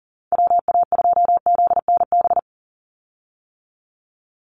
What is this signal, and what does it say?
Morse code, heard as a keyed tone, about 37 wpm, 700 Hz. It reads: WA2ZNB
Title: USA Call Signs 40wpm